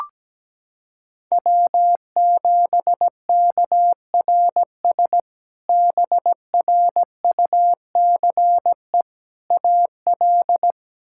Morse code, heard as a keyed tone, about 17 wpm, 700 Hz. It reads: W7KRS BRUCE AL
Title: K1USN SST QSO Practice - 17wpm